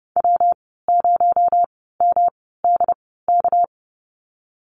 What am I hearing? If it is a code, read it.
Morse code, heard as a keyed tone, about 30 wpm, spaced as for 17 wpm, 700 Hz. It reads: W0MDK
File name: Lesson 37g - Letter x - Call Signs 30wpm